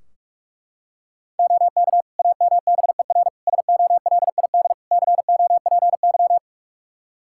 Morse code, heard as a keyed tone, about 41 wpm, 700 Hz. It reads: OK AMBER SOLID COPY
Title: Practice QSO - Exchange 3 - More Info - 45wpm